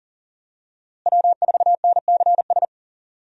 Morse code, heard as a keyed tone, about 40 wpm, 700 Hz. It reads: W4NCS
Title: USA Call Signs - 5 Characters - Repeated 3x - 40wpm